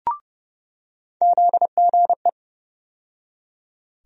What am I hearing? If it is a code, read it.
Morse code, heard as a keyed tone, about 31 wpm, 700 Hz. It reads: ZGE
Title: USA Call Sign Postfixes - 3 Letters 30wpm